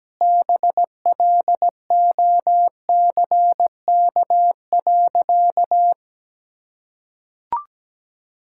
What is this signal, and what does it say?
Morse code, heard as a keyed tone, about 17 wpm, 700 Hz. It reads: BLOCK.
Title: Sentences from Top 1600 Words 17wpm